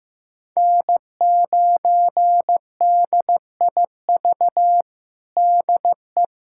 Morse code, heard as a keyed tone, about 15 wpm, 700 Hz. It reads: N9DIV DE
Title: USA Callsign DE Callsign - 5 Characters - 15wpm